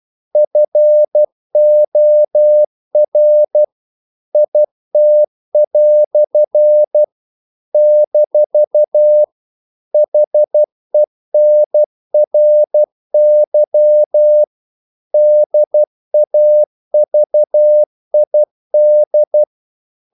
Morse code, heard as a keyed tone, about 12 wpm, 600 Hz. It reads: FOR IT" - HENRY DAVID
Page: English quotations in CW (telegraphy - morse)